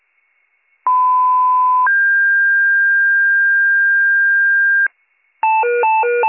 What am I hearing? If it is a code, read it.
Two-tone page, heard as a one-second tone, then a three-second tone, about 1010 then 1600 Hz. It is heard from 0.9 to 4.9 s.